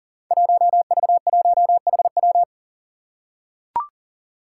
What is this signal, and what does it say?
Morse code, heard as a keyed tone, about 43 wpm, 700 Hz. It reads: 1V1HW
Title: Five Challenging Letter-Number Combinations 40wpm